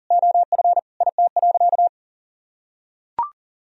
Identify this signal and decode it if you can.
Morse code, heard as a keyed tone, about 40 wpm, 700 Hz. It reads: OF IT.
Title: Sentences from Top 1000 Words - Review Part 3 - 40wpm